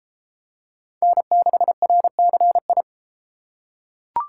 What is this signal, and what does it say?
Morse code, heard as a keyed tone, about 32 wpm, 700 Hz. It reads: N6RCI